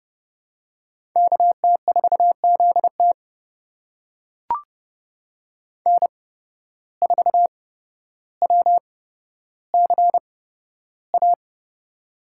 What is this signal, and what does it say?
Morse code, heard as a keyed tone, about 30 wpm, spaced as for 13 wpm, 700 Hz. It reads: KT4ZT N4WCA
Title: USA Call Signs - 5 Characters - 8x Character Spacing - 30wpm